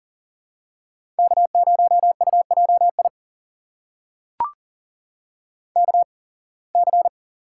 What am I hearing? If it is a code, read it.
Morse code, heard as a keyed tone, about 40 wpm, spaced as for 26 wpm, 700 Hz. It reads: K0UJI KC
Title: USA Call Signs - 8x Character Spacing - 40wpm